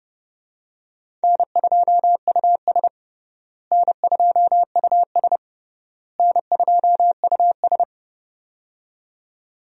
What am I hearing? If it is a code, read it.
Morse code, heard as a keyed tone, about 30 wpm, 700 Hz. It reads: N2US N2US N2US
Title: USA Call Signs - 4 Characters - Repeated 3x - 30wpm